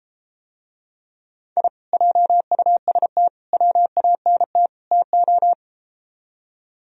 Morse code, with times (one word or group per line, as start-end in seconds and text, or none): 1.6-1.7 I
1.9-3.3 JUST
3.5-4.7 WANT
4.9-5.5 TO